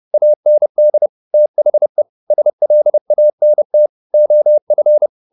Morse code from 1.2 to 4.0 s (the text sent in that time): THE SLANT